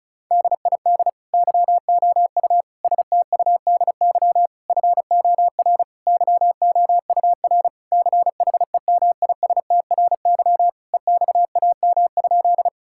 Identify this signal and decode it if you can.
Morse code, heard as a keyed tone, about 36 wpm, 700 Hz. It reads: DID YOU STUDY FOR YOUR CHEMISTRY EXAM?
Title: Sentences from Top 1000 Words - Review Part 6 - 35wpm